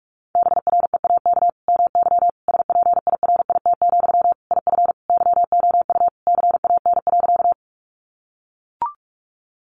Morse code, heard as a keyed tone, about 44 wpm, 700 Hz. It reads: BREAK MY SPIRIT, IF YOU CAN.
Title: Sentences from Top 1400 Words 45wpm